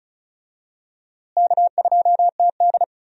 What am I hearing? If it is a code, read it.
Morse code, heard as a keyed tone, about 36 wpm, 700 Hz. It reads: K2TD
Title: USA Call Signs - 4 Characters - Repeated 2x - 35wpm